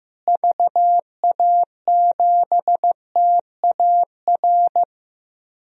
Morse code, heard as a keyed tone, about 15 wpm, 700 Hz. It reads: VA7TAR
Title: DXCC-100 Call Signs - 6 Characters - Repeated 3x - 15wpm